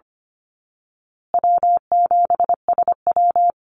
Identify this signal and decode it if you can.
Morse code, heard as a keyed tone, about 25 wpm, 700 Hz. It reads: W7SW
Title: Call Signs Repeated 2x - 25wpm